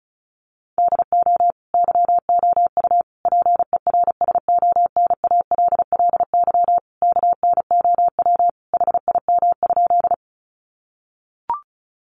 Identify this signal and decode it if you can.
Morse code, heard as a keyed tone, about 34 wpm, 700 Hz. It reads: DO YOU PERSONALLY KNOW HIM?
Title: Sentences from Top 1400 Words 35wpm